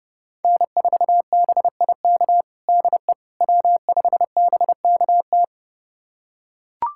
Morse code, heard as a keyed tone, about 30 wpm, 700 Hz. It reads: N4BIK DE W5BKT
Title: USA Callsign DE Callsign - 5 Characters - 30wpm